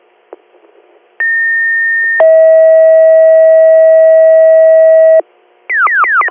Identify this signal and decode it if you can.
Two-tone page, heard as a one-second tone, then a three-second tone, about 1800 then 640 Hz. It is heard from 1.2 to 5.2 s.